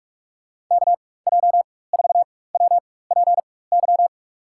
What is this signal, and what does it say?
Morse code, heard as a keyed tone, about 45 wpm, spaced as for 21 wpm, 700 Hz. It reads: KJ4WPY
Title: USA Call Signs - 6 Characters - 4x Character Spacing - 45wpm